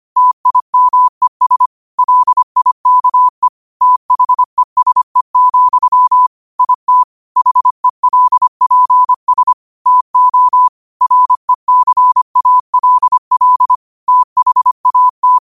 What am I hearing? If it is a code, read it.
Morse code, heard as a keyed tone, about 25 wpm, 1000 Hz. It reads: TIMES LIKE THESE, IT HELPS TO RECALL THAT
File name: qotdfiles / 25wpm